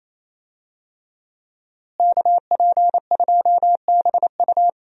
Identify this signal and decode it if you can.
Morse code, heard as a keyed tone, about 28 wpm, 700 Hz. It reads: KP2BU
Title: USA Call Signs - 5 Characters - Repeated 3x - 28wpm